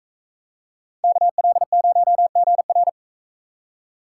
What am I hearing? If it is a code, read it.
Morse code, heard as a keyed tone, about 39 wpm, 700 Hz. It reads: KL0GR